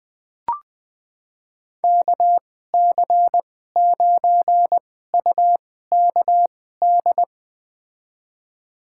Morse code, heard as a keyed tone, about 20 wpm, spaced as for 14 wpm, 700 Hz. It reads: KC9UKD
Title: USA Call Signs - 6 Characters - 2x Character Spacing - 20wpm